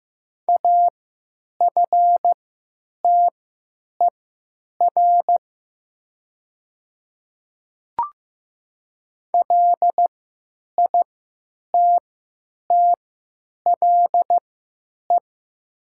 Morse code, heard as a keyed tone, about 15 wpm, spaced as for 9 wpm, 700 Hz. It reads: AFTER LITTLE
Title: 3x Character Spacing - Top 300 Words - 15wpm